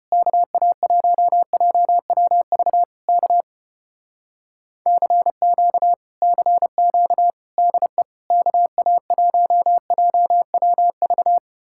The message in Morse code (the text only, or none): KA1JWV K CQ CQ DE KA1JWV